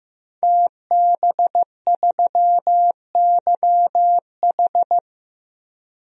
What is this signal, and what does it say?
Morse code, heard as a keyed tone, about 15 wpm, 700 Hz. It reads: TB3YH
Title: DXCC-100 Call Signs - 5 Characters - 15wpm